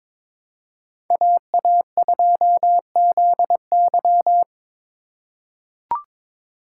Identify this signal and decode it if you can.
Morse code, heard as a keyed tone, about 22 wpm, 700 Hz. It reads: AA2ZY
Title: USA Call Signs - 5 Characters - 6x Character Spacing - 22wpm